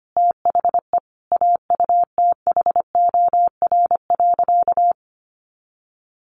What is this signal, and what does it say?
Morse code, heard as a keyed tone, about 25 wpm, 700 Hz. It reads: THE AUTHOR.
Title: Sentences from Top 1200 Words 25wpm